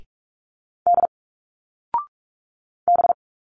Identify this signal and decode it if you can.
Morse code, heard as a keyed tone, about 46 wpm, 700 Hz. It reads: D B
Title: Single Letters 45wpm